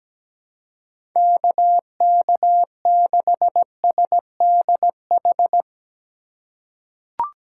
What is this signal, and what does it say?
Morse code, heard as a keyed tone, about 17 wpm, 700 Hz. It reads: KK6SDH
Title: USA Call Signs - 6 Characters - 3x Character Spacing - 17wpm